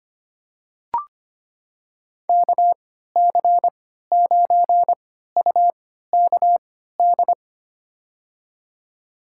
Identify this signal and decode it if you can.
Morse code, heard as a keyed tone, about 25 wpm, spaced as for 14 wpm, 700 Hz. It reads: KC9UKD
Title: USA Call Signs - 6 Characters - 3x Character Spacing - 25wpm